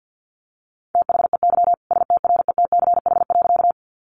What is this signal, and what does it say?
Morse code, heard as a keyed tone, about 50 wpm, 700 Hz. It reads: THEY STRETCH.
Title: Easy Phrases 50wpm